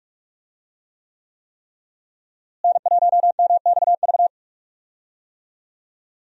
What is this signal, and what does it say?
Morse code, heard as a keyed tone, about 46 wpm, 700 Hz. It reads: N1MXV
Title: USA Call Signs - 5 Characters - Repeated 3x - 45wpm